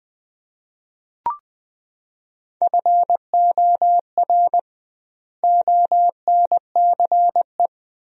Morse code, heard as a keyed tone, about 20 wpm, 700 Hz. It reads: FOR ONCE
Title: 2x Word-Spacing - Sentences from Top 300 Words - 20wpm